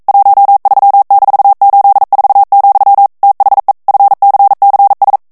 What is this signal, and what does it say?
Morse code, heard as a keyed tone, about 42 wpm, 800 Hz. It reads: 13-84, THE FCCS